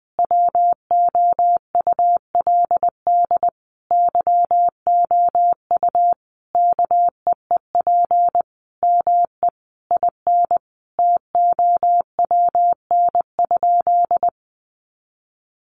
Morse code, heard as a keyed tone, about 20 wpm, 700 Hz. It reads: WOULD YOU KEEP ME IN TOWN?
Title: Sentences from Top 500 Words 20wpm